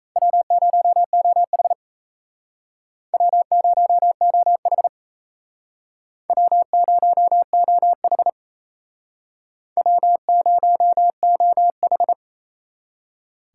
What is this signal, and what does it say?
Morse code, heard as a keyed tone, about 37 wpm, 700 Hz. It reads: W0OH W0OH W0OH W0OH